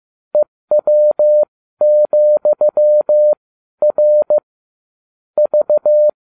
Morse code, heard as a keyed tone, about 15 wpm, spaced as for 12 wpm, 600 Hz. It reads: EW,R V